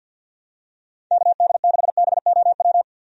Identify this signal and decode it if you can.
Morse code, heard as a keyed tone, about 52 wpm, 700 Hz. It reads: KD6BOW
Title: USA Call Signs - 6 Characters - 50wpm